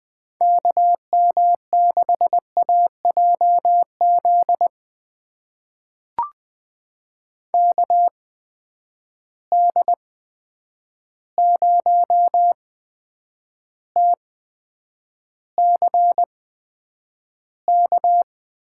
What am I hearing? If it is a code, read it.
Morse code, heard as a keyed tone, about 20 wpm, spaced as for 9 wpm, 700 Hz. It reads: KM6AJZ KD0TCK
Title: USA Call Signs - 6 Characters - 8x Character Spacing - 20wpm